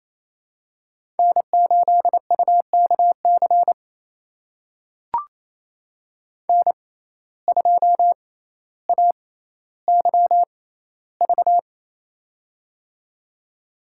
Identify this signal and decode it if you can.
Morse code, heard as a keyed tone, about 28 wpm, spaced as for 14 wpm, 700 Hz. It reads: N8UKC N2AYV
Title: USA Call Signs - 5 Characters - 6x Character Spacing - 28wpm